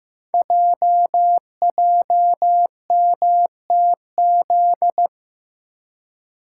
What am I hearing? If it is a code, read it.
Morse code, heard as a keyed tone, about 15 wpm, 700 Hz. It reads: JJMTZ
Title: Code Groups - 5 Letters - 15wpm